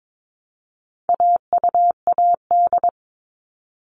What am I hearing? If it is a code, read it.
Morse code, heard as a keyed tone, about 22 wpm, 700 Hz. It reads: AUAD
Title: All Cut Numbers - 4-Digit 22wpm